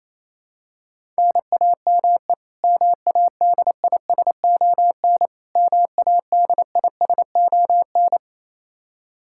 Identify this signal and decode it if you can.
Morse code, heard as a keyed tone, about 28 wpm, 700 Hz. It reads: NAME MADISON MADISON
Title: K1USN SST QSO Practice 28wpm